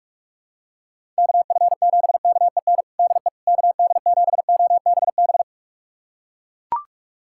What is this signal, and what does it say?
Morse code, heard as a keyed tone, about 41 wpm, 700 Hz. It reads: KF7KEN DE KD7OBB
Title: USA Callsign DE Callsign - 6 Characters - 45wpm